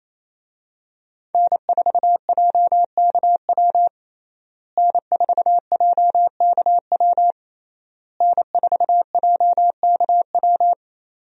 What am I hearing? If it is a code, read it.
Morse code, heard as a keyed tone, about 28 wpm, 700 Hz. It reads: N4JKW N4JKW N4JKW
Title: USA Call Signs - 5 Characters - Repeated 3x - 28wpm